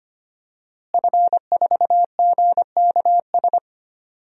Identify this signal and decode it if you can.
Morse code, heard as a keyed tone, about 25 wpm, 700 Hz. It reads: F4GKS
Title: DXCC-100 Call Signs - 5 Characters - Repeated 2x - 25wpm